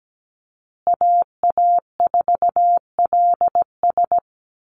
Morse code, heard as a keyed tone, about 17 wpm, 700 Hz. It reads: AA4LS
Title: Call Signs Repeated 2x - 17wpm